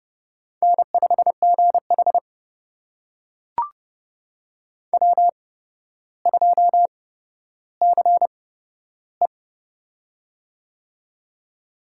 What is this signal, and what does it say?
Morse code, heard as a keyed tone, about 32 wpm, spaced as for 13 wpm, 700 Hz. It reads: N5GH W2CE
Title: USA Call Signs - 4 Characters - 8x Character Spacing - 30wpm